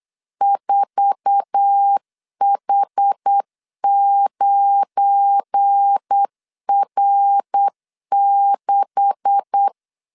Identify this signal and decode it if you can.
Morse code, heard as a keyed tone, about 8 wpm, 800 Hz. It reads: 4H9R6
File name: CW / DET / 1 9 1500C S B 8wpm 5624